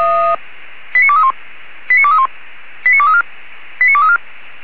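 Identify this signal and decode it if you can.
Five-tone selective call (ZVEI), heard as quick runs of five tones, tones of about 70 ms. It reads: 89231 89231 89234 89234
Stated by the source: ZVEI five-tone selective call received by SDR.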